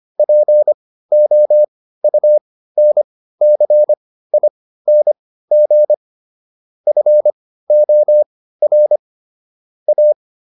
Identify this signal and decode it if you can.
Morse code, heard as a keyed tone, about 25 wpm, spaced as for 15 wpm, 600 Hz. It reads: POUNCING FOR A